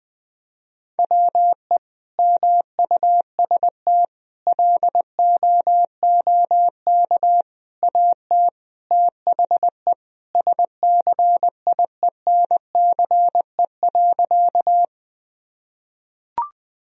Morse code, heard as a keyed tone, about 20 wpm, 700 Hz. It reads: WE MUST LOOK AT THE SCIENCE.
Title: Sentences from Top 1000 Words - Review Part 2 - 20wpm